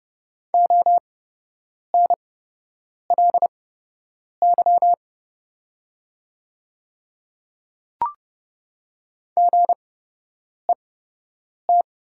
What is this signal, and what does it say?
Morse code, heard as a keyed tone, about 29 wpm, spaced as for 8 wpm, 690 Hz. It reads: ONLY GET
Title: 8x Character Spacing - Top 300 Words - 30wpm